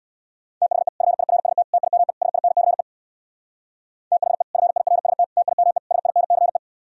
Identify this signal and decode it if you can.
Morse code, heard as a keyed tone, about 64 wpm, 700 Hz. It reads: THE FIRST TIME SINCE THE FIRST TIME SINCE